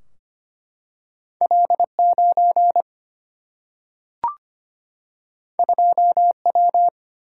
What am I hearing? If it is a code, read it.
Morse code, heard as a keyed tone, about 25 wpm, 700 Hz. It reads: L9 2W
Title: Single-Letter Single-Number 25wpm